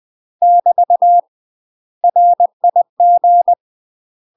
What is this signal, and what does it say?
Morse code, heard as a keyed tone, about 20 wpm, 700 Hz. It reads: <BT> RIG